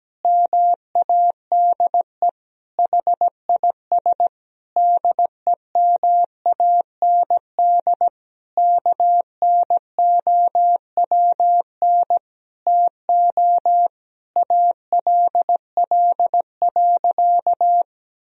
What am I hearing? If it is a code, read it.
Morse code, heard as a keyed tone, about 17 wpm, 700 Hz. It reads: MADE HIS DEMAND KNOWN TO ALL.
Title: Sentences from Top 1000 Words - Review Part 4 - 17wpm